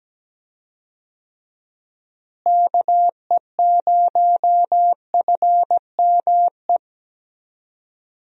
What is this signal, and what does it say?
Morse code, heard as a keyed tone, about 17 wpm, 700 Hz. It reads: KE0FME
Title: USA Call Signs - 6 Characters - Repeated 3x - 17wpm